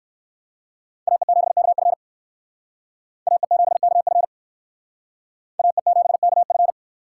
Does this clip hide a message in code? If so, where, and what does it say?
Yes: Morse code from 1.1 to 6.7 s, heard as a keyed tone, about 57 wpm, 700 Hz. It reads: AE7KF AE7KF AE7KF